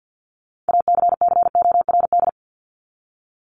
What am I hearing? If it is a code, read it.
Morse code, heard as a keyed tone, about 50 wpm, 700 Hz. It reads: ACCORD
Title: ING Suffix - Top 100 Words with Base Words 50wpm